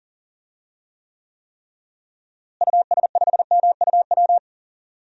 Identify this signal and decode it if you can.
Morse code, heard as a keyed tone, about 38 wpm, 700 Hz. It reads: US5MUW
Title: DXCC-100 Call Signs - 6 Characters - Repeated 3x - 40wpm